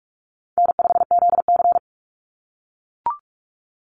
Morse code, heard as a keyed tone, about 44 wpm, 700 Hz. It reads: N5ZC
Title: Call Signs - 45wpm